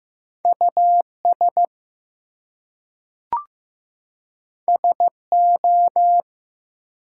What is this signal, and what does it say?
Morse code, heard as a keyed tone, about 15 wpm, 700 Hz. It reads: US SO
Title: Two Letter Words - 15wpm